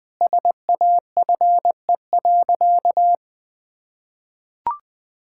Morse code, heard as a keyed tone, about 21 wpm, 700 Hz. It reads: SAFE.
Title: Sentences from Top 1600 Words 20wpm